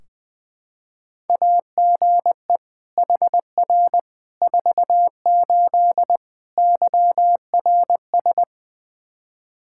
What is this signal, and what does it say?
Morse code, heard as a keyed tone, about 20 wpm, 700 Hz. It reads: AGE HR 48 YRS
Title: Practice QSO - Exchange 3 - More Info - 20wpm